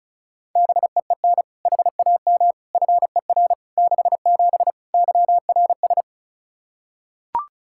Morse code, heard as a keyed tone, about 34 wpm, 700 Hz. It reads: BEEN HAM FER 67 YRS
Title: Practice QSO - Exchange 3 - More Info - 35wpm